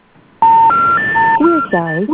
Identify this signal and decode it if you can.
Special information tone, with speech.